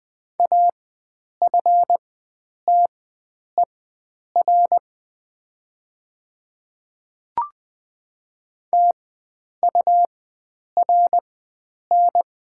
Morse code, heard as a keyed tone, about 20 wpm, spaced as for 9 wpm, 700 Hz. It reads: AFTER TURN
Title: 4x Character Spacing - Top 300 Words - 20wpm